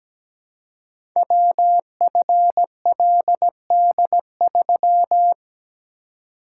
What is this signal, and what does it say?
Morse code, heard as a keyed tone, about 17 wpm, 700 Hz. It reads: WFLD3
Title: Code Groups - 5 Letter:Numbers - 17wpm